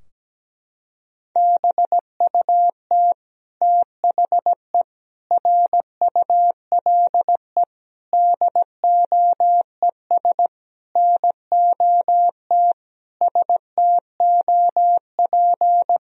Morse code, heard as a keyed tone, about 17 wpm, 700 Hz. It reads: BUT THE RULE DOES NOT STOP
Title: Sentences from Top 1000 Words - Review Part 2 - 17wpm